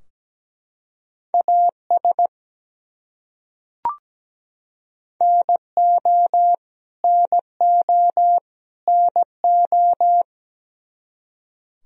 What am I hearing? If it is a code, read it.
Morse code, heard as a keyed tone, about 17 wpm, 700 Hz. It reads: AS NO NO NO
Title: Top 100 Words Repeated 17wpm